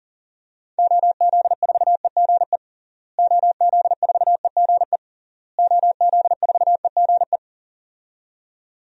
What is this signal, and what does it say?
Morse code, heard as a keyed tone, about 40 wpm, 700 Hz. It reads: OZ4EGE OZ4EGE OZ4EGE
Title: DXCC-100 Call Signs - 6 Characters - Repeated 3x - 40wpm